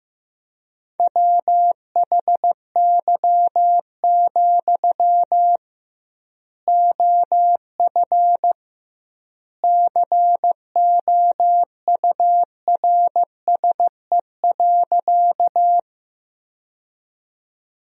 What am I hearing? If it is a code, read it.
Morse code, heard as a keyed tone, about 16 wpm, 700 Hz. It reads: WHY, OF COURSE.
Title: 2x Word-Spacing - Sentences from Top 500 Words - 15wpm